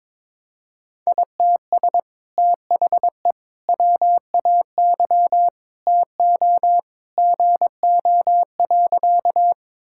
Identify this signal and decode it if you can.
Morse code, heard as a keyed tone, about 22 wpm, 700 Hz. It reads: ITS THE WAY TO GO.
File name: practice / 22wpm / Sentences with 2-Word Phrases - 22wpm